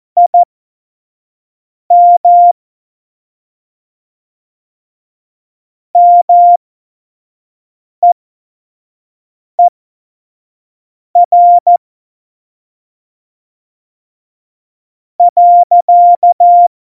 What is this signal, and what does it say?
Morse code, heard as a keyed tone, about 14 wpm, spaced as for 5 wpm, 700 Hz. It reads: IM MEER .
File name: Morsekurs / Witze / Witz 26. Juni 2018, 25 F70